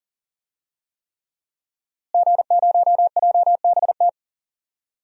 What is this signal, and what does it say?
Morse code, heard as a keyed tone, about 40 wpm, 700 Hz. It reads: G0JBT
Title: DXCC-100 Call Signs - 5 Characters - Repeated 3x - 40wpm